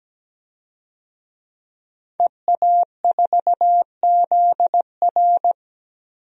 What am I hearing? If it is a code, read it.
Morse code, heard as a keyed tone, about 17 wpm, 700 Hz. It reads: EA4ZR
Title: DXCC-100 Call Signs - 5 Characters - 17wpm